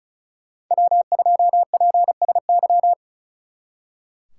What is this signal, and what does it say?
Morse code, heard as a keyed tone, about 47 wpm, spaced as for 41 wpm, 700 Hz. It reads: W2PSY
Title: Speed Racing - USA 1x3 Call Signs 35wpm